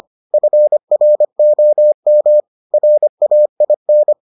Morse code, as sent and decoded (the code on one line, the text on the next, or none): ..-. .-. --- -- / .-. .- .. -.
FROM RAIN